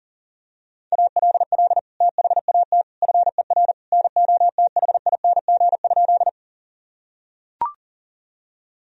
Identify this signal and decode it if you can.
Morse code, heard as a keyed tone, about 40 wpm, 700 Hz. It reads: ALL THAT FER NOTHING?
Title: Sentences from Top 500 Words and Abbreviations 40wpm